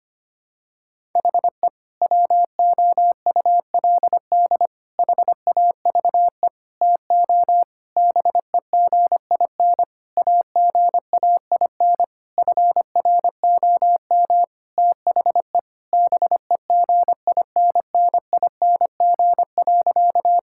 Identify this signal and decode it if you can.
Morse code, heard as a keyed tone, about 25 wpm, 700 Hz. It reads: HE WOULD HAVE TO BEGIN AGAIN FROM THE BEGINNING.
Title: Sentences from Top 1000 Words - Review Part 5 - 25wpm